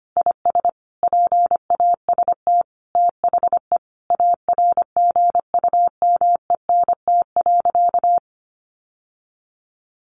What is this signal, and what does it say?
Morse code, heard as a keyed tone, about 25 wpm, 700 Hz. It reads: IS PAST THE ARGUMENT.
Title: Sentences from Top 1400 Words 25wpm